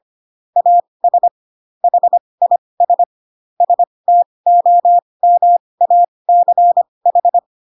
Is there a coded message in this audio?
Yes: Morse code from 0.6 to 7.4 s, heard as a keyed tone, about 25 wpm, spaced as for 20 wpm, 700 Hz. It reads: AS HIS STOMACH